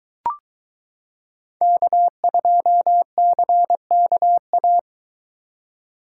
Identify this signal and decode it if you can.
Morse code, heard as a keyed tone, about 23 wpm, 700 Hz. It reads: K2CKA